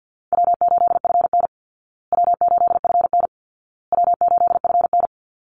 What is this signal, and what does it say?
Morse code, heard as a keyed tone, about 50 wpm, 700 Hz. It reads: W8FN W8FN W8FN
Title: Call Signs Repeated 2x - 50wpm